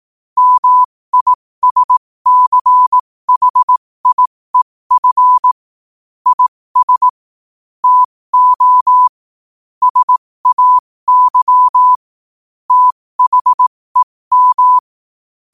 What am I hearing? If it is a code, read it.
Morse code, heard as a keyed tone, about 18 wpm, spaced as for 15 wpm, 1000 Hz. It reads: MISCHIEF IS TO SAY THEM
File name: qotdfiles / 15wpm